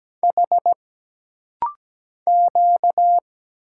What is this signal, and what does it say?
Morse code, heard as a keyed tone, about 17 wpm, 700 Hz. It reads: H Q
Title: Single Letter-Number - Rapid-Fire 17wpm